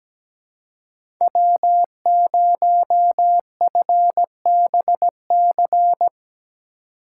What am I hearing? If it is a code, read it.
Morse code, heard as a keyed tone, about 17 wpm, 700 Hz. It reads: W0FBC
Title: USA Call Signs - 4x Character Spacing - 17wpm